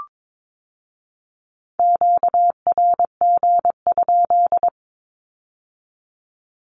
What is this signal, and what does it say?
Morse code, heard as a keyed tone, about 22 wpm, 700 Hz. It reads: QRG?
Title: Q-Codes with Spoken Meaning - 22wpm